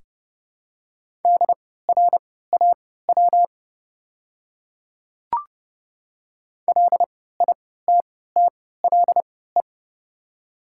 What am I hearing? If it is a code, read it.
Morse code, heard as a keyed tone, about 30 wpm, spaced as for 17 wpm, 700 Hz. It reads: DRAW LITTLE
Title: 3x Character Spacing - Top 500 Words - 30wpm